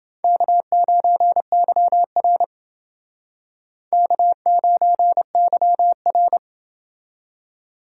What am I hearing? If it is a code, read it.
Morse code, heard as a keyed tone, about 29 wpm, 700 Hz. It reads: K9YR K9YR